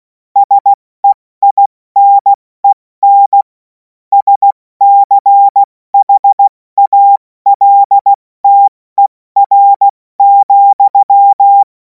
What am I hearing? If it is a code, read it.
Morse code, heard as a keyed tone, about 16 wpm, spaced as for 14 wpm, 800 Hz. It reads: SEINEN SCHALTER,
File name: Morsekurs / Witze / Witz 31. März 2026 70 F80